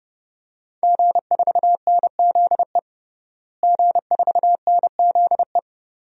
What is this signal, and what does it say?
Morse code, heard as a keyed tone, about 32 wpm, 700 Hz. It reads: G4NZE G4NZE
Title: DXCC-100 Call Signs - 5 Characters - Repeated 2x - 30wpm